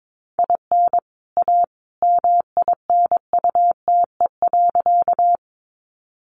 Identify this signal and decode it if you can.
Morse code, heard as a keyed tone, about 22 wpm, 700 Hz. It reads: IN A MINUTE.
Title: Sentences from Top 500 Words 22wpm